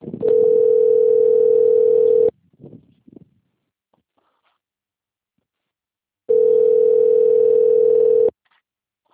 Ringback tone (North America), 440 + 480 Hz.